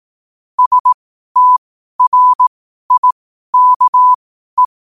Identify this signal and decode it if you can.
Morse code, heard as a keyed tone, about 18 wpm, spaced as for 12 wpm, 1000 Hz. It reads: STRIKE